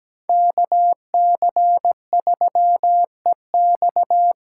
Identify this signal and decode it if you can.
Morse code, heard as a keyed tone, about 17 wpm, 700 Hz. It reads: KC3EX
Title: USA Call Signs - 6 Characters - 8x Character Spacing - 17wpm